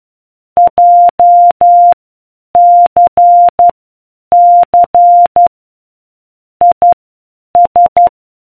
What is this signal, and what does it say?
Morse code, heard as a keyed tone, about 12 wpm, spaced as for 8 wpm, 700 Hz. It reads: JCC IS